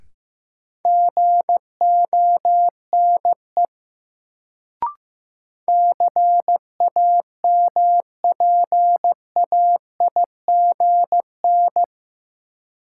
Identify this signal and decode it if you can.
Morse code, heard as a keyed tone, about 15 wpm, 700 Hz. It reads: GONE CAMPAIGN
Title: Sentences from Top 1200 Words - New Word Drill Alt - 15wpm